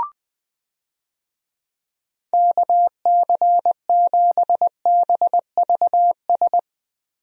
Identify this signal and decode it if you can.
Morse code, heard as a keyed tone, about 21 wpm, 700 Hz. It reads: KC7BVS
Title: USA Call Signs - 6 Characters - Repeated 2x - 20wpm